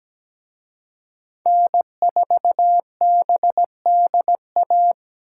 Morse code, heard as a keyed tone, about 20 wpm, 700 Hz. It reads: N4BDA
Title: Speed Racing - USA 1x3 Call Signs 17wpm